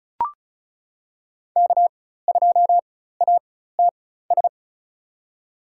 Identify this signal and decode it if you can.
Morse code, heard as a keyed tone, about 36 wpm, spaced as for 16 wpm, 700 Hz. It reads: K2ATS
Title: USA Call Signs - 5 Characters - 4x Character Spacing - 35wpm